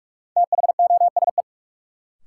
Morse code, heard as a keyed tone, about 46 wpm, 700 Hz. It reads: THOSE
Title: Sentences from Top 300 Words - New Word Drill - 45wpm